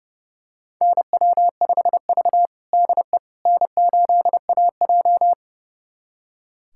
Morse code, heard as a keyed tone, about 31 wpm, 700 Hz. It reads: NW5V DE N8AJ
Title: USA Callsign DE Callsign - 4 Characters - 30wpm